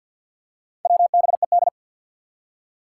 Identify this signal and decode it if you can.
Morse code, heard as a keyed tone, about 53 wpm, 700 Hz. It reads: WBED
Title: Four Challenging Letter-Number Combinations 50wpm